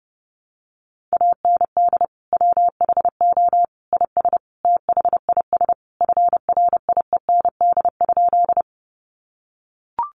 Morse code, heard as a keyed tone, about 30 wpm, 700 Hz. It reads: AND WHO IS THIS FRIEND?
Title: Sentences from Top 300 Words 30wpm